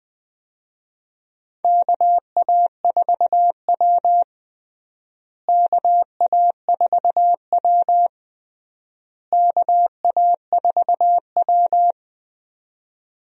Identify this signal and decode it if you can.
Morse code, heard as a keyed tone, about 20 wpm, 700 Hz. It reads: KA4W KA4W KA4W
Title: USA Call Signs - 4 Characters - Repeated 3x - 20wpm